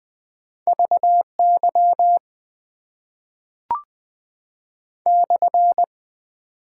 Morse code, heard as a keyed tone, about 20 wpm, 700 Hz. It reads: VY /
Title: Top 90 QSO Elements - 6x Character Spacing - 20wpm